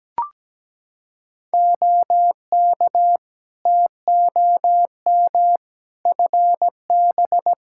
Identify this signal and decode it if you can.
Morse code, heard as a keyed tone, about 17 wpm, 700 Hz. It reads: OK TOM FB
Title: Practice QSO - Exchange 2 - Info - 17wpm